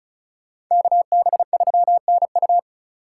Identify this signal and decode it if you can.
Morse code, heard as a keyed tone, about 44 wpm, 700 Hz. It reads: KB3NU
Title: Speed Racing - USA 2x2 Call Signs 35wpm